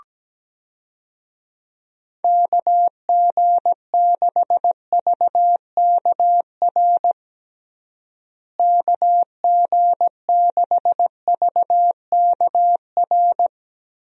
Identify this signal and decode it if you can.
Morse code, heard as a keyed tone, about 17 wpm, 700 Hz. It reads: KG6VKR KG6VKR
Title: USA Call Signs - 6 Characters - Repeated 3x - 17wpm